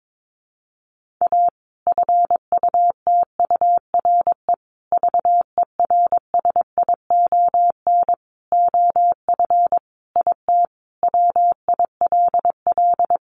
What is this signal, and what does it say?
Morse code, heard as a keyed tone, about 22 wpm, 700 Hz. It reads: A FUTURE VERSION OF IT WILL
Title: Sentences from Top 1400 Words 22wpm